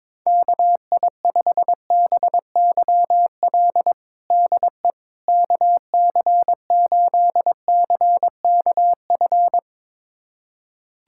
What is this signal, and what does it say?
Morse code, heard as a keyed tone, about 22 wpm, 700 Hz. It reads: KI5BYL DE KC8CKF
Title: USA Callsign DE Callsign - 6 Characters - 22wpm